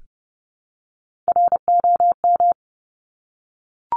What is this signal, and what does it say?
Morse code, heard as a keyed tone, about 30 wpm, 700 Hz. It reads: ROM
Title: Top Three Letter Combinations in Words 30wpm